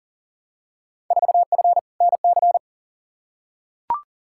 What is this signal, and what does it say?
Morse code, heard as a keyed tone, about 40 wpm, 700 Hz. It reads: 4F NC
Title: Field Day Exchanges - 40wpm